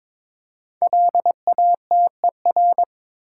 Morse code, heard as a keyed tone, about 22 wpm, 700 Hz. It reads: LATER
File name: 5 Letter Words - First 3 Letters Given - 22wpm